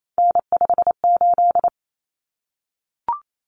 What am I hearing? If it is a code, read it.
Morse code, heard as a keyed tone, about 28 wpm, 700 Hz. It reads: N58
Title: Common Cut Numbers - 3-Digit 28wpm